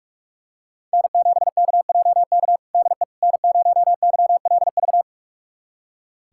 Morse code, heard as a keyed tone, about 46 wpm, 700 Hz. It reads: N7KJK DE N0YLV
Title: USA Callsign DE Callsign - 5 Characters - 45wpm